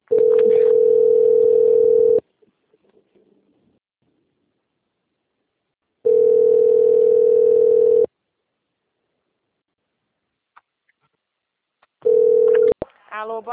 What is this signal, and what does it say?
Ringback tone (North America), 440 + 480 Hz, with speech.
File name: out- - - - - .52531